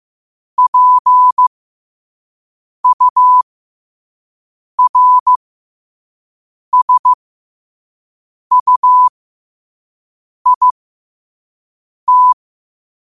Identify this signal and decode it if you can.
Morse code, heard as a keyed tone, about 15 wpm, spaced as for 5 wpm, 1000 Hz. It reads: PURSUIT
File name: qotdfiles / 5wpm